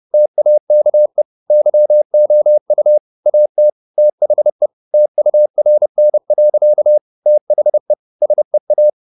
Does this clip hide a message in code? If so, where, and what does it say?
Yes: Morse code from 0.1 to 8.9 s, heard as a keyed tone, about 30 wpm, 600 Hz. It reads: TAKE YOU AT THE TURN. THE SEA